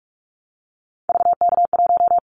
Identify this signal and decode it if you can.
Morse code, heard as a keyed tone, about 46 wpm, 700 Hz. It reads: VK1
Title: Callsign Prefixes for Top 100 DXCC - 45wpm